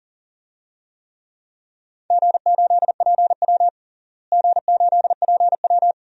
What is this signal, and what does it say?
Morse code, heard as a keyed tone, about 40 wpm, 700 Hz. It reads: G8PW G8PW
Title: DXCC-100 Call Signs - 4 Characters - Repeated 2x - 40wpm